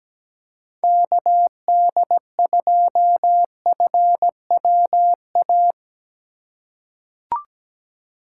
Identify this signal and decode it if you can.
Morse code, heard as a keyed tone, about 17 wpm, 700 Hz. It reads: KD2FWA
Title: USA Call Signs - 6 Characters - 3x Character Spacing - 17wpm